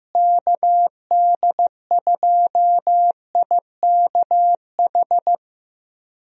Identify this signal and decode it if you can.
Morse code, heard as a keyed tone, about 15 wpm, 700 Hz. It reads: KD2IKH
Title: USA Call Signs - 6 Characters - Repeated 2x - 15wpm